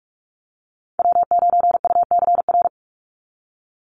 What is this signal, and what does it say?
Morse code, heard as a keyed tone, about 44 wpm, 700 Hz. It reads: W9UCR
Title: Call Signs Repeated 3x - 45wpm